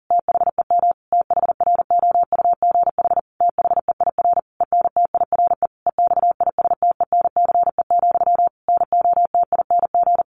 Morse code, heard as a keyed tone, about 40 wpm, 700 Hz. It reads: THEM THROUGH THEIR ENTIRE EXISTENCE, NOTING
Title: A Sound Of Thunder 40wpm